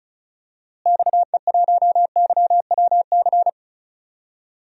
Morse code, heard as a keyed tone, about 35 wpm, 700 Hz. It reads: XE1YWC
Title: DXCC-100 Call Signs - 6 Characters - Repeated 3x - 35wpm